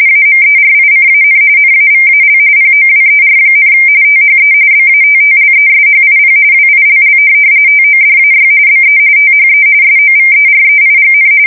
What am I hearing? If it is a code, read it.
RTTY, about 45.45 baud. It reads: AMERICAN NAVAL POWER, THE BATTLESHIP WILL TODAY BE THE SOVEREIGN
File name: fsk170 / USS-New-Jersey